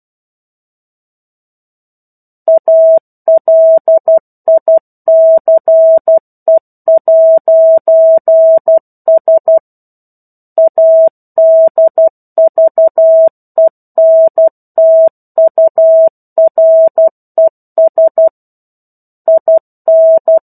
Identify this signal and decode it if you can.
Morse code, heard as a keyed tone, about 12 wpm, 650 Hz. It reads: ALICE'S ADVENTURES IN
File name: 01-Chapter-01 12 WPM